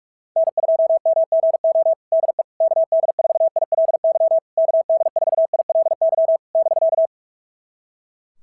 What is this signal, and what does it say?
Morse code, heard as a keyed tone, about 44 wpm, 690 Hz. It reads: N2MGO DE KD4ILY KD4ILY <BK>
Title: SOTA QSO Practice 45wpm